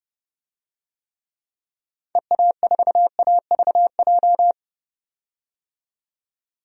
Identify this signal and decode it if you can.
Morse code, heard as a keyed tone, about 30 wpm, 700 Hz. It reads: EA4AVJ
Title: DXCC-100 Call Signs - 6 Characters - 30wpm